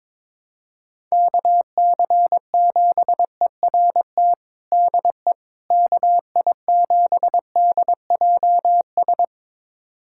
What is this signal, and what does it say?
Morse code, heard as a keyed tone, about 22 wpm, 700 Hz. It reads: KC7ERT DE KI7DJS
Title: USA Callsign DE Callsign - 6 Characters - 22wpm